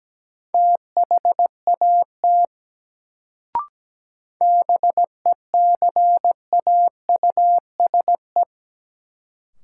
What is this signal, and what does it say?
Morse code, heard as a keyed tone, about 17 wpm, 700 Hz. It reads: THAT BECAUSE
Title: Sets of 2 Words - New Word Drill - 17wpm